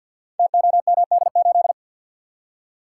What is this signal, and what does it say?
Morse code, heard as a keyed tone, about 46 wpm, 700 Hz. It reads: TOMD8
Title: Code Groups - 5 Letter:Numbers - 50wpm